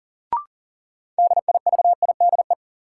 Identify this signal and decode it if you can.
Morse code, heard as a keyed tone, about 40 wpm, 700 Hz. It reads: DIVIDE
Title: Sentences from Top 1600 Words - New Word Drill - 40wpm